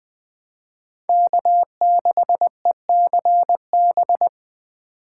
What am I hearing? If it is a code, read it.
Morse code, heard as a keyed tone, about 23 wpm, 700 Hz. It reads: K6ECB
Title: Speed Racing - USA 1x3 Call Signs 20wpm